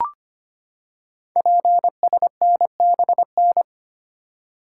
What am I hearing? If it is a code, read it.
Morse code, heard as a keyed tone, about 25 wpm, 700 Hz. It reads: PSNBN
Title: Code Groups - 5 Letters - 25wpm